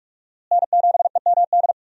Morse code, heard as a keyed tone, about 47 wpm, 700 Hz. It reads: N7EMD
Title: USA Call Signs - 5 Characters - Repeated 3x - 45wpm